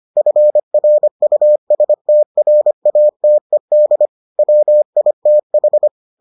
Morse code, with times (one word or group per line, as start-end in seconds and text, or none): 0.2-4.1 FRUSTRATED
4.4-5.9 WITH